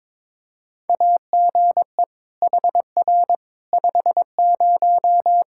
Morse code, heard as a keyed tone, about 21 wpm, 700 Hz. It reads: AGE HR 50
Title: K1USN SST QSO Practice 22wpm